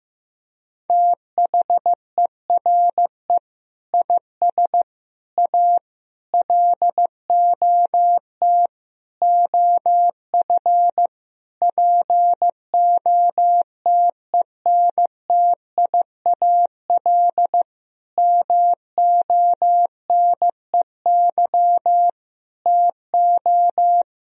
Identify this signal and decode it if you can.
Morse code, heard as a keyed tone, about 16 wpm, 700 Hz. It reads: THERE IS A LOT OF POTENTIAL MONEY TO
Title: Sentences from Top 1000 Words - Review Part 4 - 15wpm